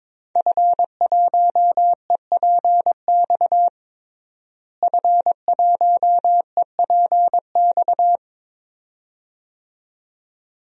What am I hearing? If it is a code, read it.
Morse code, heard as a keyed tone, about 22 wpm, 700 Hz. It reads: F1EPX F1EPX
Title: DXCC-100 Call Signs - 5 Characters - Repeated 2x - 22wpm